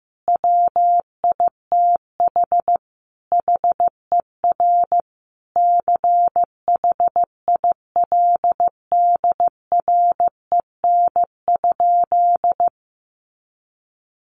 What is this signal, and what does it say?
Morse code, heard as a keyed tone, about 15 wpm, 700 Hz. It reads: WITH HER CHILDREN?
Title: Sentences from Top 1500 Words 15wpm